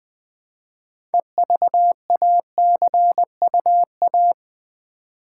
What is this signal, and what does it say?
Morse code, heard as a keyed tone, about 20 wpm, 700 Hz. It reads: EVACUA
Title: Build A Word - 2 Letters At A Time - 8-Letter Words - 20wpm